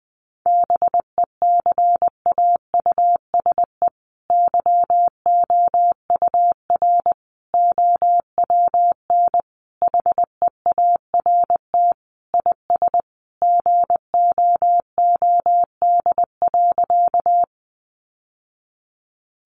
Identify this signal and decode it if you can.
Morse code, heard as a keyed tone, about 20 wpm, 700 Hz. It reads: BECAUSE YOUR OWN HEART IS GOOD.
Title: Sentences from Top 1200 Words 20wpm